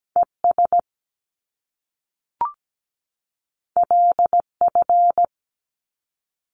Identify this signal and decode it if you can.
Morse code, heard as a keyed tone, about 17 wpm, 700 Hz. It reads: ES LF
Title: Top Two Letter Combinations in Words 17wpm